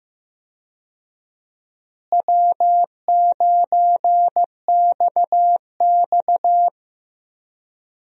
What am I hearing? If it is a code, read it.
Morse code, heard as a keyed tone, about 15 wpm, 700 Hz. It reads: W9XX
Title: USA Call Signs - 4 Characters - Repeated 3x - 15wpm